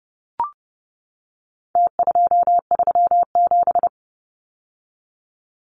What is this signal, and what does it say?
Morse code, heard as a keyed tone, about 30 wpm, 700 Hz. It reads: T237
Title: Numbers - Contest Style - 4-Digit 30wpm